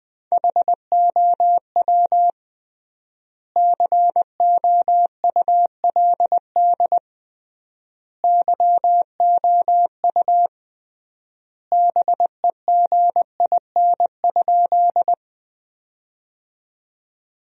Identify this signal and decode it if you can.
Morse code, heard as a keyed tone, about 20 wpm, 700 Hz. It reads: HOW COULD YOU BEGIN?
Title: 3x Word-Spacing - Sentences from Top 300 Words - 20wpm